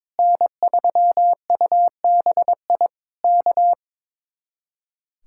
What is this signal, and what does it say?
Morse code, heard as a keyed tone, about 22 wpm, 700 Hz. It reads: N3UBI K
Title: Speed-Racing Common QSO Phrases - 22wpm